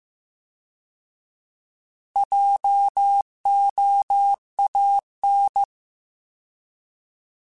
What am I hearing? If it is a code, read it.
Morse code, heard as a keyed tone, about 15 wpm, 800 Hz. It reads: JOAN
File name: Alphabetical / Alphabetical mp3s / Js @13